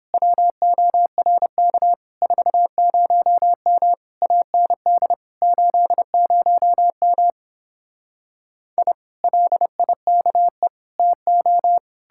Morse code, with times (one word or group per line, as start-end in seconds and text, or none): 0.1-1.9 WORK
2.2-3.9 40M
4.2-5.1 AND
5.4-7.3 80M
8.8-8.9 I
9.2-10.7 LIKE
11.0-11.8 TO